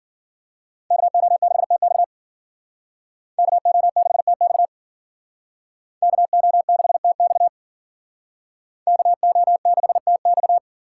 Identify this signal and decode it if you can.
Morse code, heard as a keyed tone, about 52 wpm, 700 Hz. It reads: KO6TX KO6TX KO6TX KO6TX